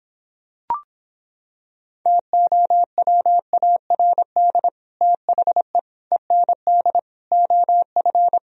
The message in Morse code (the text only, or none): TOWARD THE END OF